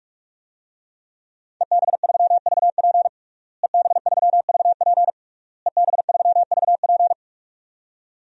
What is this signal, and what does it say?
Morse code, heard as a keyed tone, about 41 wpm, 700 Hz. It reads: EB3VP EB3VP EB3VP
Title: DXCC-100 Call Signs - 5 Characters - Repeated 3x - 45wpm